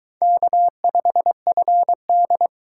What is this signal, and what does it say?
Morse code, heard as a keyed tone, about 22 wpm, 700 Hz. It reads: K5FD